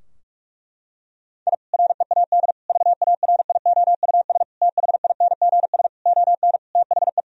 Morse code, heard as a keyed tone, about 46 wpm, 700 Hz. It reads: I READ VARIOUS THINGS ON THE
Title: Sentences from Top 1000 Words - Review Part 4 - 45wpm